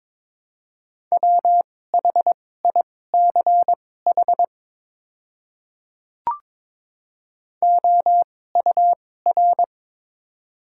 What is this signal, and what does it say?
Morse code, heard as a keyed tone, about 22 wpm, spaced as for 16 wpm, 700 Hz. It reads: WHICH OUR
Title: 2x Character Spacing - Top 200 Words - 22wpm